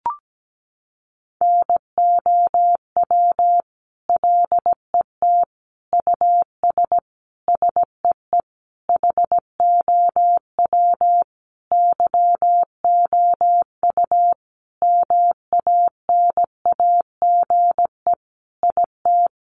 Morse code, heard as a keyed tone, about 17 wpm, 700 Hz. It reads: NOW LET US SEE HOW YOU MANAGE IT
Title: Sentences from Top 1400 Words 17wpm